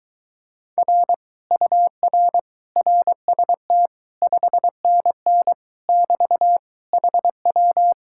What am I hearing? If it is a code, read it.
Morse code, heard as a keyed tone, about 22 wpm, 700 Hz. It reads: R UR RST 5NN <BT> HW